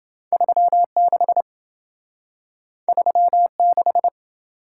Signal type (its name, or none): Morse code